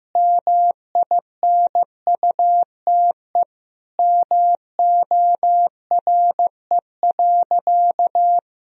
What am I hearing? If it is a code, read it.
Morse code, heard as a keyed tone, about 15 wpm, 700 Hz. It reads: MINUTE MORE.
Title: Sentences from Top 1000 Words - Review Part 4 - 15wpm